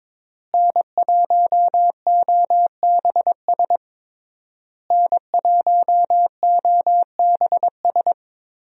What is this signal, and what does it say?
Morse code, heard as a keyed tone, about 22 wpm, 700 Hz. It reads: N1OBS N1OBS
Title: USA Call Signs - 5 Characters - Repeated 3x - 22wpm